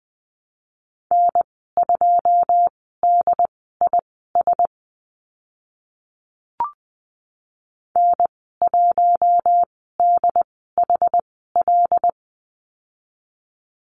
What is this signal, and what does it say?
Morse code, heard as a keyed tone, about 20 wpm, spaced as for 15 wpm, 700 Hz. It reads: N2DIS N1DHL
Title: 088 Lesson 15g - Number 5 - Call Signs 20wpm